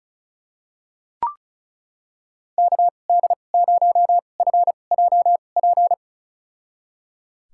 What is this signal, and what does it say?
Morse code, heard as a keyed tone, about 36 wpm, spaced as for 25 wpm, 700 Hz. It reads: KD0FJP
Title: USA Call Signs - 6 Characters - 2x Character Spacing - 35wpm